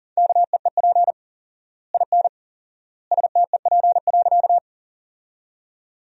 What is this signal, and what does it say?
Morse code, heard as a keyed tone, about 40 wpm, 700 Hz. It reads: KEEP IN STEP.
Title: 4x Word-Spacing - Sentences from Top 400 Words - 40wpm